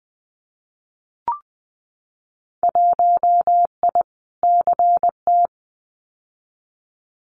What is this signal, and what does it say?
Morse code, heard as a keyed tone, about 20 wpm, 700 Hz. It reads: 1I CT
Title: Winter Field Day Exchanges 20wpm